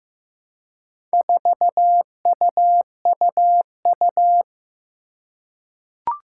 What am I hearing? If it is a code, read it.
Morse code, heard as a keyed tone, about 15 wpm, 700 Hz. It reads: 4UUU
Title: Four Challenging Letter-Number Combinations 15wpm